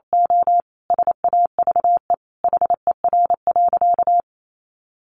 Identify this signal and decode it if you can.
Morse code, heard as a keyed tone, about 28 wpm, 700 Hz. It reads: O SAVE HER.
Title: Sentences from Top 1100 Words 28wpm